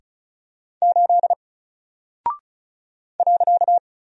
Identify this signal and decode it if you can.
Morse code, heard as a keyed tone, about 36 wpm, 700 Hz. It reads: 8 .
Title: Single Character (Letter:Number:Punctuation:Prosign) 35wpm